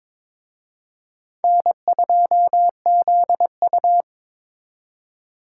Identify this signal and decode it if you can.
Morse code, heard as a keyed tone, about 26 wpm, 700 Hz. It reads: N2ZU
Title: Speed Racing - USA 1x2 Call Signs 22wpm